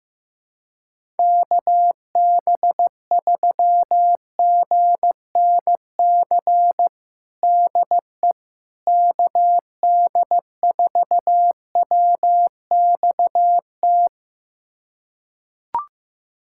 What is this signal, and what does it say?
Morse code, heard as a keyed tone, about 15 wpm, 700 Hz. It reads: KB3GNC DE KD4WXT
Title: USA Callsign DE Callsign - 6 Characters - 15wpm